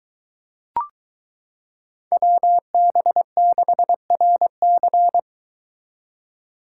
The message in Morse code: WB6RC

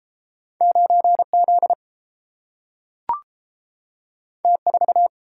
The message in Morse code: 9Z T4

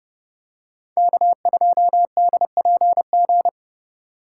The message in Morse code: K2DPG

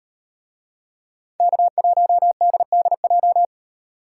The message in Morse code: K1DDJ

